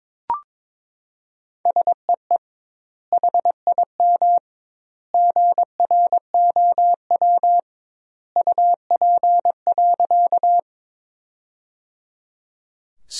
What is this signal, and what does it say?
Morse code, heard as a keyed tone, about 21 wpm, 700 Hz. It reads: SEE HIM GROW UP.
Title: 2x Word-Spacing - Sentences from Top 300 Words - 22wpm